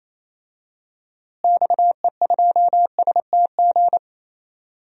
Morse code, heard as a keyed tone, about 27 wpm, 700 Hz. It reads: XE2STG
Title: DXCC-100 Call Signs - 6 Characters - Repeated 2x - 28wpm